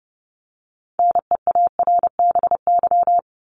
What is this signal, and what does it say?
Morse code, heard as a keyed tone, about 30 wpm, 700 Hz. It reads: NEARBY
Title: Top 1501-1750 Words 30wpm